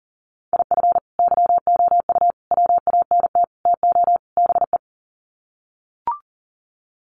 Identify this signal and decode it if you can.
Morse code, heard as a keyed tone, about 40 wpm, 700 Hz. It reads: IF YOU WANT TO BE
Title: Sets of 5 Words Encore 40wpm